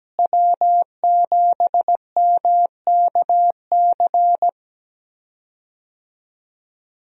Morse code, heard as a keyed tone, about 17 wpm, 700 Hz. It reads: W7MKC
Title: USA Call Signs - 5 Characters - Repeated 3x - 17wpm